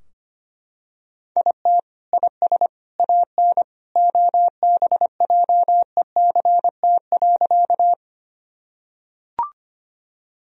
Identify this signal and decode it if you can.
Morse code, heard as a keyed tone, about 25 wpm, 700 Hz. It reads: IT IS AN OBJECT.
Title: Sentences from Top 1000 Words - Review Part 3 - 25wpm